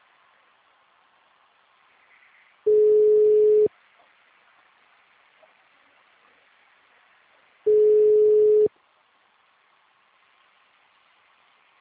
Ringback tone (Europe, 425 Hz).
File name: out- -2004- - - .81